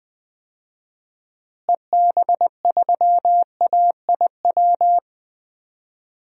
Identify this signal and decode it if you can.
Morse code, heard as a keyed tone, about 20 wpm, 700 Hz. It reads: EB3AIW
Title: DXCC-100 Call Signs - 6 Characters - Repeated 3x - 20wpm